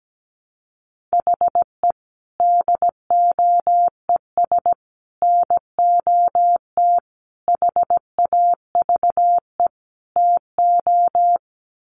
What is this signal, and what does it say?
Morse code, heard as a keyed tone, about 17 wpm, 700 Hz. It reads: HE DOES NOT HAVE TO
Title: Sets of 5 Words Encore 17wpm